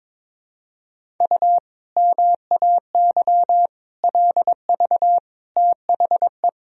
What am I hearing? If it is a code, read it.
Morse code, heard as a keyed tone, about 22 wpm, 700 Hz. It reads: U MAY LV THE
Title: Sentences from Top 400 Words and Abbreviations 22wpm